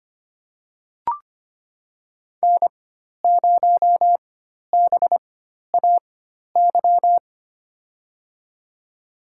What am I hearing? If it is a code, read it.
Morse code, heard as a keyed tone, about 25 wpm, spaced as for 12 wpm, 700 Hz. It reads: N0BAY
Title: USA Call Signs - 5 Characters - 4x Character Spacing - 25wpm